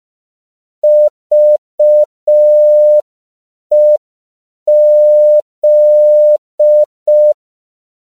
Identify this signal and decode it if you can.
Morse code, heard as a keyed tone, about 5 wpm, 600 Hz. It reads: VEZ
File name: CW / 5PPM